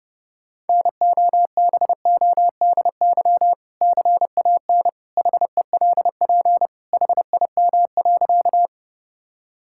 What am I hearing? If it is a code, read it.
Morse code, heard as a keyed tone, about 30 wpm, 700 Hz. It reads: NOBODY CAN HELP HIM.
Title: Sentences from Top 1600 Words 30wpm